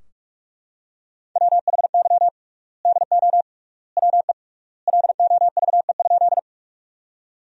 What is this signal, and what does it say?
Morse code, heard as a keyed tone, about 41 wpm, 700 Hz. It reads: WHY DO WE LOVE?
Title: 3x Word-Spacing - Sentences from Top 400 Words - 45wpm